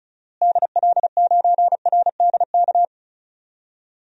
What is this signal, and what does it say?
Morse code, heard as a keyed tone, about 34 wpm, 700 Hz. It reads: DL9RDK
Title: DXCC-100 Call Signs - 6 Characters - Repeated 3x - 35wpm